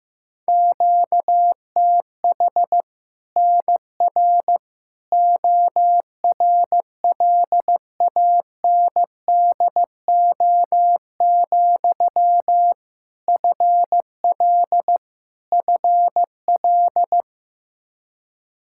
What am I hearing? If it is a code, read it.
Morse code, heard as a keyed tone, about 15 wpm, 700 Hz. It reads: QTH NR ORLANDO, FL FL
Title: Practice QSO - Exchange 1 - Beginning - 15wpm